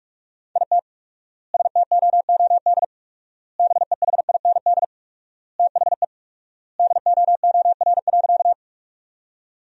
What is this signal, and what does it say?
Morse code, heard as a keyed tone, about 46 wpm, 700 Hz. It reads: IT STOOD BEHIND THE DOOR.
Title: 4x Word-Spacing - Sentences from Top 500 Words - 45wpm